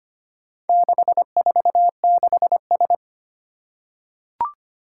Morse code, heard as a keyed tone, about 25 wpm, 700 Hz. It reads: 646S
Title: Four Challenging Letter-Number Combinations 25wpm